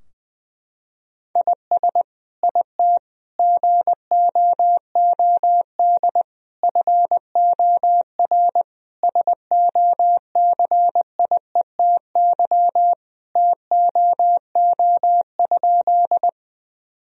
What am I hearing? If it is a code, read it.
Morse code, heard as a keyed tone, about 20 wpm, 700 Hz. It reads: IS IT GOOD FOR SOCIETY TOO?
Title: Sentences from Top 1000 Words - Review Part 4 - 20wpm